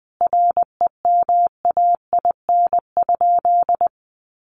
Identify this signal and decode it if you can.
Morse code, heard as a keyed tone, about 20 wpm, 700 Hz. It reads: REMAIN?
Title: Sentences from Top 1200 Words 20wpm